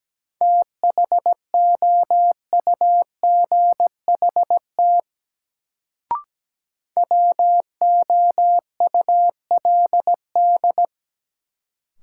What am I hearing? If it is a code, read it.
Morse code, heard as a keyed tone, about 17 wpm, 700 Hz. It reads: THOUGHT WOULD
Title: Sets of 2 Words - New Word Drill - 17wpm